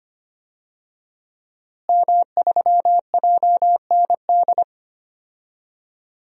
Morse code, heard as a keyed tone, about 25 wpm, 700 Hz. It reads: M3JND
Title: DXCC-100 Call Signs - 5 Characters - Repeated 3x - 25wpm